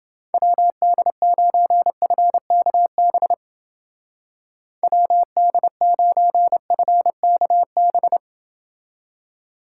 Morse code, heard as a keyed tone, about 28 wpm, 700 Hz. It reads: WD9FKB WD9FKB